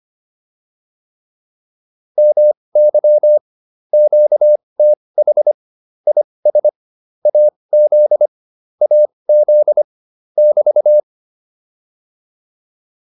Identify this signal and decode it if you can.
Morse code, heard as a keyed tone, about 25 wpm, spaced as for 20 wpm, 600 Hz. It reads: MY QTH IS AZ AZ <BT>